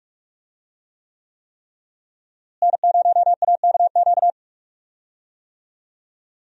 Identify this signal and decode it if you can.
Morse code, heard as a keyed tone, about 41 wpm, 700 Hz. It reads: N0AKQ
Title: USA Call Signs - 5 Characters - Repeated 3x - 45wpm